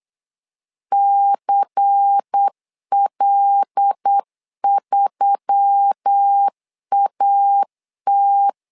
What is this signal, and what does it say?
Morse code, heard as a keyed tone, about 8 wpm, 800 Hz. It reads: CL3AT
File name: LEZIONI / 5 LBVC / 5 LBVC 8wpm 2935